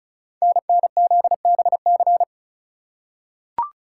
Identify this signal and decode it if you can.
Morse code, heard as a keyed tone, about 36 wpm, 690 Hz. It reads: NNZBC
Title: Code Groups - 5 Letters - 35wpm